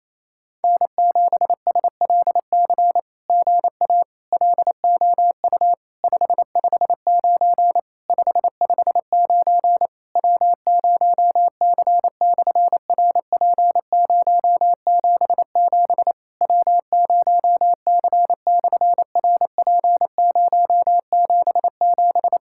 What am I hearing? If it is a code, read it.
Morse code, heard as a keyed tone, about 27 wpm, 700 Hz. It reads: N7SLC GA LOU 559 559 W0C/RP077 W0C/RP077
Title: SOTA QSO Practice 28wpm